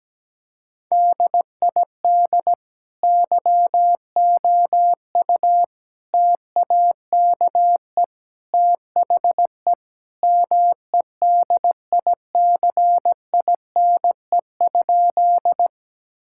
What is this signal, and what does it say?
Morse code, heard as a keyed tone, about 17 wpm, 700 Hz. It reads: DID YOU TAKE THE MEDICINE?
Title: Sentences from Top 1000 Words - Review Part 6 - 17wpm